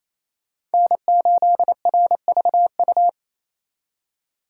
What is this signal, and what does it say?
Morse code, heard as a keyed tone, about 34 wpm, 700 Hz. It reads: N8RVU
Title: Speed Racing - USA 1x3 Call Signs 28wpm